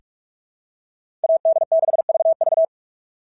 Morse code, heard as a keyed tone, about 46 wpm, 680 Hz. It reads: AD6VV
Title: SOTA QSO Practice 45wpm